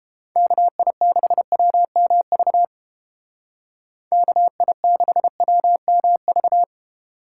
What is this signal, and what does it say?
Morse code, heard as a keyed tone, about 32 wpm, 700 Hz. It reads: KI6WMV KI6WMV